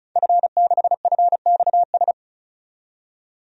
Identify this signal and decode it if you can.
Morse code, heard as a keyed tone, about 36 wpm, 700 Hz. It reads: F6FXS
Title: DXCC-100 Call Signs - 5 Characters - 35wpm